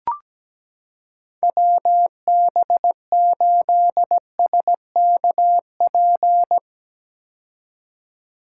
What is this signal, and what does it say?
Morse code, heard as a keyed tone, about 17 wpm, 700 Hz. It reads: WB8SKP
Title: K1USN SST QSO Practice - 17wpm